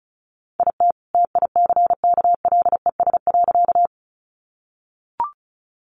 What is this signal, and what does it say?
Morse code, heard as a keyed tone, about 34 wpm, 700 Hz. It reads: IT TICKLES.
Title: Additional Easy Phrases 35wpm